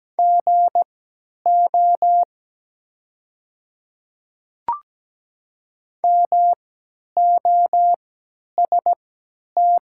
Morse code, heard as a keyed tone, about 17 wpm, spaced as for 10 wpm, 700 Hz. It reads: GO MOST
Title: 3x Character Spacing - Top 100 Words - 17wpm